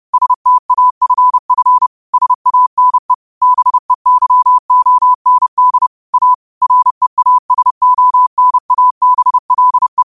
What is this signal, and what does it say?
Morse code, heard as a keyed tone, about 30 wpm, 1000 Hz. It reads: STAFF SANE BEYOND A REASONABLE
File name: qotdfiles / 30wpm